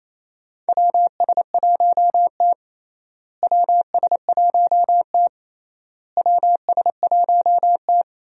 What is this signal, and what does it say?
Morse code, heard as a keyed tone, about 28 wpm, 700 Hz. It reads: WS1T WS1T WS1T
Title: USA Call Signs - 4 Characters - Repeated 3x - 28wpm